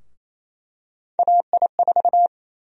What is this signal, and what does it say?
Morse code, heard as a keyed tone, about 28 wpm, 700 Hz. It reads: AI4
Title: USA Call Sign Prefixes - 28wpm